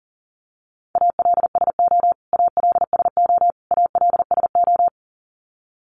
Morse code, heard as a keyed tone, about 40 wpm, 700 Hz. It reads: ALSO ALSO ALSO
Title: Top 500 Words Repeated 40wpm